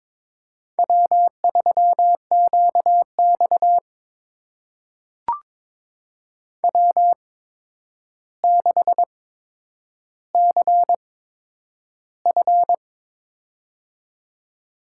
Morse code, heard as a keyed tone, about 22 wpm, spaced as for 9 wpm, 700 Hz. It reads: W3QX W6CF
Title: USA Call Signs - 4 Characters - 8x Character Spacing - 22wpm